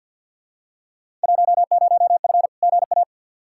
Morse code, heard as a keyed tone, about 48 wpm, 700 Hz. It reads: 10F GA
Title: Field Day Exchanges Encore - 50wpm